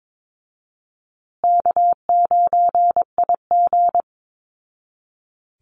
Morse code, heard as a keyed tone, about 22 wpm, 700 Hz. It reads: K9IG
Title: Call Signs Repeated 2x - 22wpm